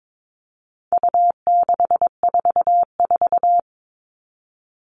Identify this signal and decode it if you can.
Morse code, heard as a keyed tone, about 22 wpm, 700 Hz. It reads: U644
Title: All Cut Numbers - 4-Digit 22wpm